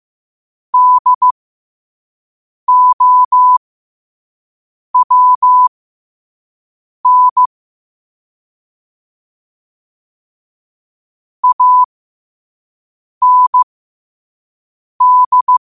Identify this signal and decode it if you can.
Morse code, heard as a keyed tone, about 15 wpm, spaced as for 5 wpm, 1000 Hz. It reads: DOWN AND